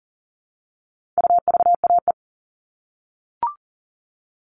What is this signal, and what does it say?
Morse code, heard as a keyed tone, about 40 wpm, 700 Hz. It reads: UVAE
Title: All Cut Numbers - 4-Digit 40wpm